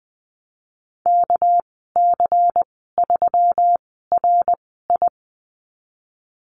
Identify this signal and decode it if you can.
Morse code, heard as a keyed tone, about 20 wpm, spaced as for 15 wpm, 700 Hz. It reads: KC3RI
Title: 200 Lesson 32g - Letter k - Call Signs 20wpm